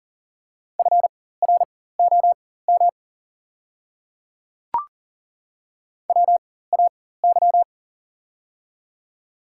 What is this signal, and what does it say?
Morse code, heard as a keyed tone, about 40 wpm, spaced as for 19 wpm, 700 Hz. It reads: FROM WAY
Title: 4x Character Spacing - Top 100 Words - 40wpm